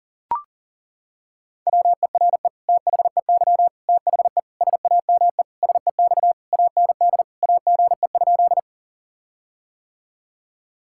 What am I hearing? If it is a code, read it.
Morse code, heard as a keyed tone, about 40 wpm, 700 Hz. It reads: WERE THEY THE SAME SEX AND AGE?
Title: Sentences from Top 1600 Words 40wpm